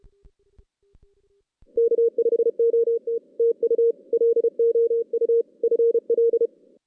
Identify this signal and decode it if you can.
Morse code, heard as a keyed tone, about 32 wpm, 460 Hz. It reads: K5OT TU LOU FL
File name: 2016 JAN CW NAQP / 2016 Jan naqp k5ot - 0-6